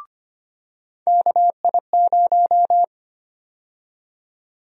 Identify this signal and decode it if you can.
Morse code, heard as a keyed tone, about 25 wpm, 700 Hz. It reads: KI0
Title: USA Call Sign Prefixes - 25wpm